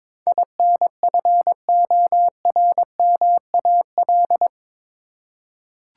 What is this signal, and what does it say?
Morse code, heard as a keyed tone, about 22 wpm, 700 Hz. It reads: INFORMAL
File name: practice / 22wpm / Eight Letter Words - 22wpm